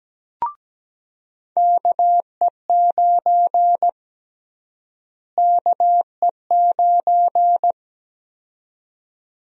Morse code, heard as a keyed tone, about 17 wpm, 700 Hz. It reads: KE9 KE9
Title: USA Call Sign Prefixes - Repeated 2x - 17wpm